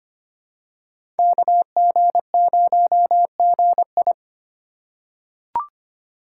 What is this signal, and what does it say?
Morse code, heard as a keyed tone, about 25 wpm, 700 Hz. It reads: KG0GI
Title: USA Call Signs - 5 Characters - 3x Character Spacing - 25wpm